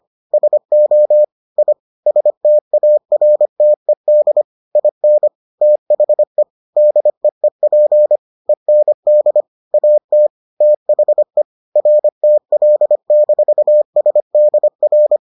Morse code, heard as a keyed tone, about 25 wpm, 600 Hz. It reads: SO I STARTED IN THE DEEP END AT THE RTL-SDR